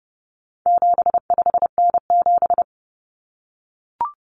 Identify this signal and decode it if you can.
Morse code, heard as a keyed tone, about 30 wpm, 700 Hz. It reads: 75N7
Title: Common Cut Numbers - 4-Digit 30wpm